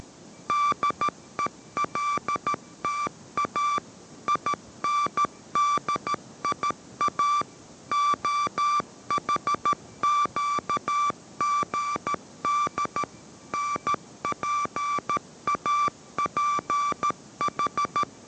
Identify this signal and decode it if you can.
Morse code, heard as a keyed tone, about 13 wpm, 1170 Hz. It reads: DELTA INDIA OHQGD NPAPH